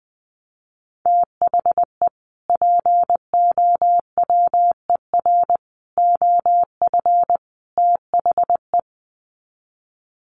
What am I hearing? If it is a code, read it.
Morse code, heard as a keyed tone, about 20 wpm, 700 Hz. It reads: THE POWER OF THE
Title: Sets of 4 Words 20wpm